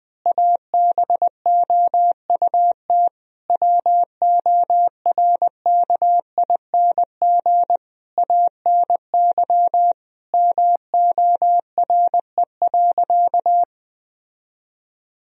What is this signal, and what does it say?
Morse code, heard as a keyed tone, about 21 wpm, 700 Hz. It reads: ABOUT WORKING ANY MORE.
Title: Sentences from Top 1000 Words - Review Part 4 - 20wpm